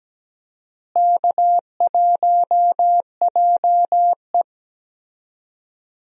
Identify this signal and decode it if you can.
Morse code, heard as a keyed tone, about 17 wpm, 700 Hz. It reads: K1JE
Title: USA Call Signs - 4 Characters - Repeated 3x - 17wpm